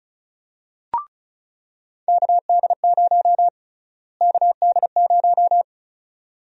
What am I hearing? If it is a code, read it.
Morse code, heard as a keyed tone, about 36 wpm, 700 Hz. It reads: KD0 KD0
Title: USA Call Sign Prefixes - Repeated 2x - 35wpm